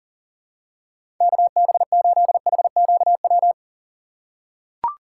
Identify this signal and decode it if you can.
Morse code, heard as a keyed tone, about 40 wpm, 700 Hz. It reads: KB8HQW
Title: USA Call Signs - 6 Characters - 4x Character Spacing - 40wpm